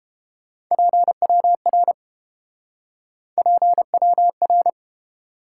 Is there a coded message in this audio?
Yes: Morse code from 0.7 to 4.7 s, heard as a keyed tone, about 32 wpm, 700 Hz. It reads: PWR PWR